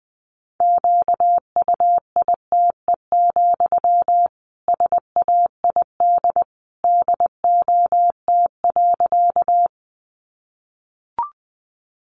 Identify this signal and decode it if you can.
Morse code, heard as a keyed tone, about 20 wpm, 700 Hz. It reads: QUITE, SAID DOT.
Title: Sentences from Top 1500 Words 20wpm